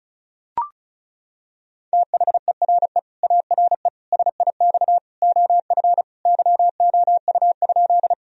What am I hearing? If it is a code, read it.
Morse code, heard as a keyed tone, about 36 wpm, 700 Hz. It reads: THERE ARE SIX OF YOU?
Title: Sentences from Top 1000 Words - Review Part 2 - 35wpm